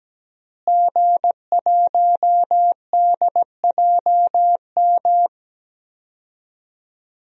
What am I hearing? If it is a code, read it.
Morse code, heard as a keyed tone, about 17 wpm, 700 Hz. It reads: G1DJM
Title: DXCC-100 Call Signs - 5 Characters - Repeated 2x - 17wpm